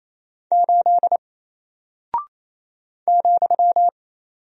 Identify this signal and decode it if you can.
Morse code, heard as a keyed tone, about 28 wpm, 700 Hz. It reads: 8 ,
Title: Single Character (Letter:Number:Punctuation:Prosign) Alt 28wpm